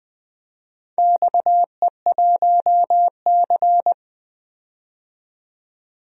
Morse code, heard as a keyed tone, about 20 wpm, 700 Hz. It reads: XE1C
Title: DXCC-100 Call Signs - 4 Characters - 20wpm